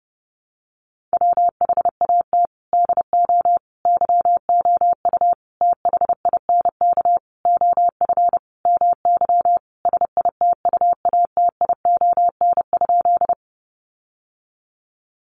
Morse code, heard as a keyed tone, about 30 wpm, 700 Hz. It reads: WHAT DO YOU THINK OF MY SITUATION?
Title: Sentences from Top 1200 Words 30wpm